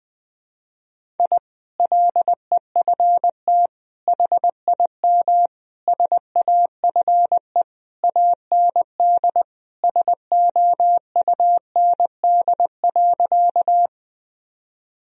Morse code, heard as a keyed tone, about 20 wpm, 700 Hz. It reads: I LEFT HIM SAFE AND SOUND.
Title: Sentences from Top 1000 Words - Review Part 1 - 20wpm